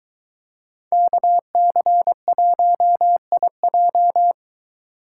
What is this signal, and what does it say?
Morse code, heard as a keyed tone, about 22 wpm, 700 Hz. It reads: KC1IJ